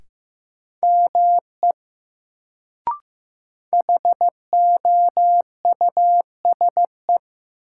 Morse code, heard as a keyed tone, about 15 wpm, 700 Hz. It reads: ME HOUSE
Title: Sentences from Top 200 Words - New Word Drill - 15wpm